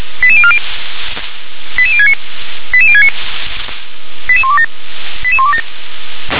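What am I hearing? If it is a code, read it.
Five-tone selective call (ZVEI), heard as quick runs of five tones, tones of about 70 ms. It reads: 88040 88069 88069 88127 88127
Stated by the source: ZVEI five-tone selective call received by SDR.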